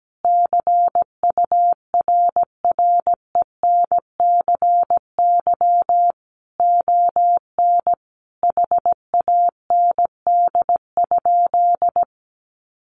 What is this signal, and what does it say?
Morse code, heard as a keyed tone, about 17 wpm, 700 Hz. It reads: CURRENCY ON HAND?
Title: Sentences from Top 1500 Words 17wpm